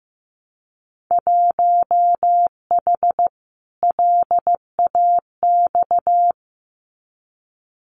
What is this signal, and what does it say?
Morse code, heard as a keyed tone, about 15 wpm, 700 Hz. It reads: 1H LAX
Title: Winter Field Day Exchanges 15wpm